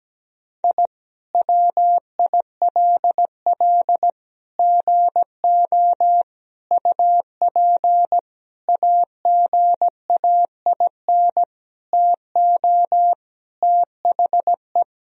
Morse code, heard as a keyed tone, about 17 wpm, 700 Hz. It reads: I WILL GO UP AGAIN TO THE
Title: Sentences from Top 1000 Words - Review Part 3 - 17wpm